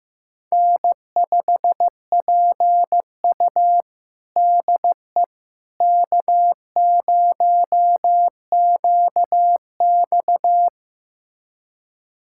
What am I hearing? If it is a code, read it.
Morse code, heard as a keyed tone, about 15 wpm, 700 Hz. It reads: N5PU DE K0QX
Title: USA Callsign DE Callsign - 4 Characters - 15wpm